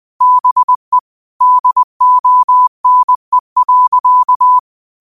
Morse code, heard as a keyed tone, about 20 wpm, 1000 Hz. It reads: BE DONE.